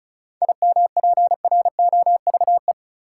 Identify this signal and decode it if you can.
Morse code, heard as a keyed tone, about 34 wpm, 700 Hz. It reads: IMPROVE
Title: Sentences from Top 800 Words - New Word Drill - 35wpm